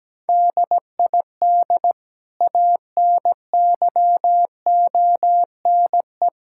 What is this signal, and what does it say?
Morse code, heard as a keyed tone, about 17 wpm, 700 Hz. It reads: DID ANYONE
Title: Sentences from Top 1600 Words 17wpm